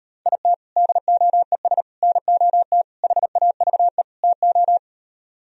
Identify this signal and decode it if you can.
Morse code, heard as a keyed tone, about 38 wpm, 700 Hz. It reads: IT DOES NOT HAVE TO